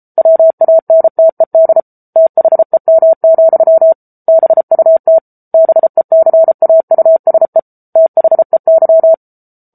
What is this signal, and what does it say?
Morse code, heard as a keyed tone, about 34 wpm, 650 Hz. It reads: WANTED THEM, BUT BECAUSE THEY
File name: Book Two / 07-Chapter 7 - 33 WPM